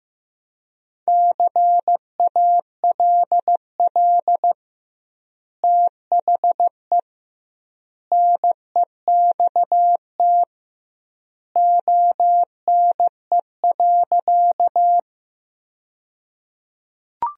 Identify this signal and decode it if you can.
Morse code, heard as a keyed tone, about 15 wpm, 700 Hz. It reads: CALL THE NEXT ONE.
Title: 2x Word-Spacing - Sentences from Top 300 Words - 15wpm